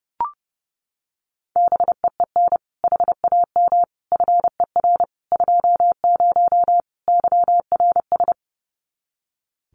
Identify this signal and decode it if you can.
Morse code, heard as a keyed tone, about 30 wpm, 700 Hz. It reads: BEEN HAM FER 20 YRS
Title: Common QSO Phrases 30wpm